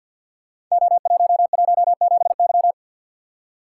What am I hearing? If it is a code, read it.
Morse code, heard as a keyed tone, about 47 wpm, 700 Hz. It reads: O117Y
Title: Code Groups - 5 Letter:Numbers - 50wpm